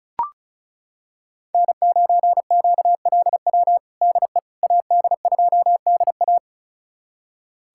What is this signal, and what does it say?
Morse code, heard as a keyed tone, about 36 wpm, 700 Hz. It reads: N9QLW DE AD2DA
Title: USA Callsign DE Callsign - 5 Characters - 35wpm